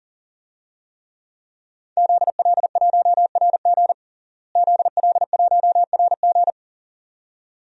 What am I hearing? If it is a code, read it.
Morse code, heard as a keyed tone, about 37 wpm, 700 Hz. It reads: ZL1RG ZL1RG
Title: DXCC-100 Call Signs - 5 Characters - Repeated 2x - 40wpm